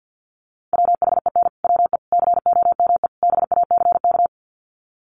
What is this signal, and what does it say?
Morse code, heard as a keyed tone, about 50 wpm, 700 Hz. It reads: WHEN WE COME BACK
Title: Sets of 4 Words 50wpm